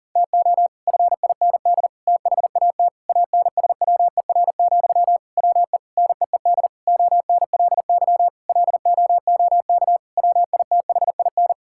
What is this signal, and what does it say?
Morse code, heard as a keyed tone, about 37 wpm, 700 Hz. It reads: TO FIND THAT ANSWER, WE NEED ONLY LOOK WITHIN
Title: Sentences from Top 1600 Words 40wpm